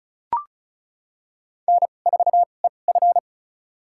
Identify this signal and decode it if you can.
Morse code, heard as a keyed tone, about 34 wpm, spaced as for 25 wpm, 700 Hz. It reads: N4EF
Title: USA Call Signs - 4 Characters - 2x Character Spacing - 35wpm